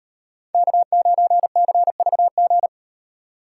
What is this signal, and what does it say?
Morse code, heard as a keyed tone, about 38 wpm, 700 Hz. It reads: K9CVG